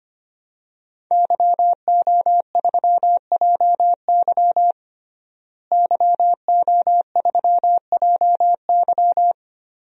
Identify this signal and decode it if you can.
Morse code, heard as a keyed tone, about 25 wpm, 700 Hz. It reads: YO3JY YO3JY
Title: DXCC-100 Call Signs - 5 Characters - Repeated 3x - 25wpm